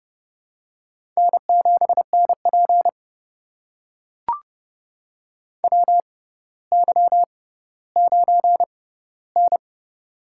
Morse code, heard as a keyed tone, about 32 wpm, spaced as for 16 wpm, 700 Hz. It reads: N7NP WY9N
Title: USA Call Signs - 4 Characters - 6x Character Spacing - 30wpm